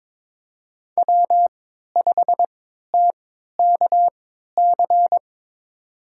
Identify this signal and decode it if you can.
Morse code, heard as a keyed tone, about 22 wpm, spaced as for 13 wpm, 700 Hz. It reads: W5TKC
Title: USA Call Signs - 5 Characters - 3x Character Spacing - 22wpm